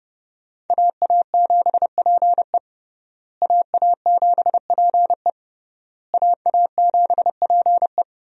Morse code, heard as a keyed tone, about 30 wpm, 700 Hz. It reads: AA7PE AA7PE AA7PE
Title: USA Call Signs - 5 Characters - Repeated 3x - 30wpm